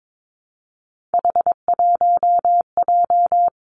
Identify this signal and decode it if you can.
Morse code, heard as a keyed tone, about 22 wpm, 700 Hz. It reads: H1J
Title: Three Challenging Letter-Number Combinations 22wpm